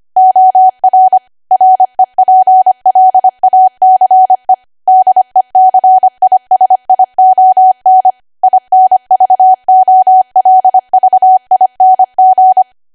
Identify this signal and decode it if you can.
Morse code, heard as a keyed tone, about 25 wpm, 750 Hz. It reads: OR REPLACE DECISION INVOLVING
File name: Archive / 25 WPM / 25WPM